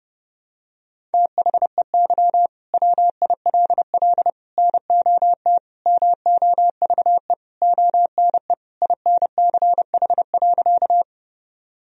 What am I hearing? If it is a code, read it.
Morse code, heard as a keyed tone, about 30 wpm, 700 Hz. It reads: THEY WILL NOT MOVE ONE INCH.
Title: Sentences from Top 1000 Words - Review Part 3 - 30wpm